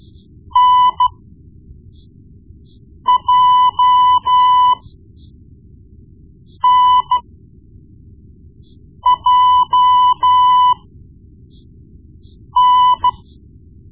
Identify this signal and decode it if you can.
Morse code, heard as a keyed tone, about 14 wpm, spaced as for 4 wpm, 1000 Hz. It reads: NJNJN
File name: Morse Lessons / Morse Lessons / Tape-1B